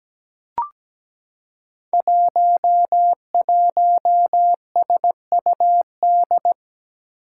Morse code, heard as a keyed tone, about 17 wpm, 700 Hz. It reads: 11SUD
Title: Five Challenging Letter-Number Combinations 17wpm